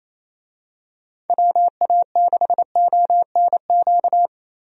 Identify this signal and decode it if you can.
Morse code, heard as a keyed tone, about 27 wpm, 700 Hz. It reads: WA6ONQ
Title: USA Call Signs - 6 Characters - Repeated 2x - 28wpm